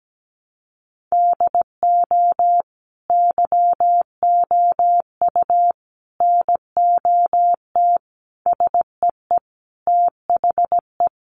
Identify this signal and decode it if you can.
Morse code, heard as a keyed tone, about 17 wpm, 700 Hz. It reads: DO YOU NOT SEE THE
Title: Sentences from Top 800 Words 17wpm